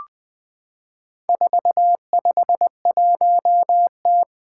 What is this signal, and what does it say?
Morse code, heard as a keyed tone, about 20 wpm, 700 Hz. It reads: 451T
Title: Four Challenging Letter-Number Combinations 20wpm